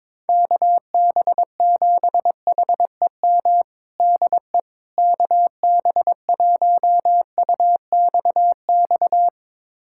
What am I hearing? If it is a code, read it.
Morse code, heard as a keyed tone, about 22 wpm, 700 Hz. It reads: KB7HEM DE KB1UXX
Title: USA Callsign DE Callsign - 6 Characters - 22wpm